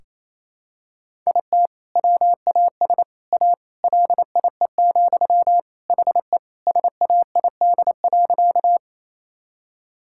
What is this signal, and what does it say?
Morse code, heard as a keyed tone, about 28 wpm, 700 Hz. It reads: IT WAS A LIE, HE SAID.
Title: Sentences from Top 1600 Words 28wpm